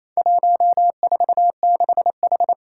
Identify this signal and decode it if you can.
Morse code, heard as a keyed tone, about 27 wpm, 700 Hz. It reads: 146H
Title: Four Challenging Letter-Number Combinations 28wpm